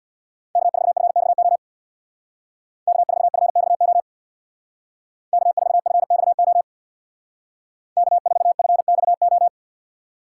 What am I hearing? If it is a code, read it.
Morse code, heard as a keyed tone, about 65 wpm, 700 Hz. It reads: K4FXO K4FXO K4FXO K4FXO